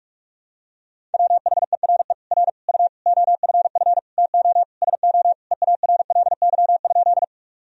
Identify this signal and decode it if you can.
Morse code, heard as a keyed tone, about 46 wpm, 700 Hz. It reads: WHERE R U OFF TO SO EARLY?
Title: Sentences from Top 400 Words and Abbreviations 45wpm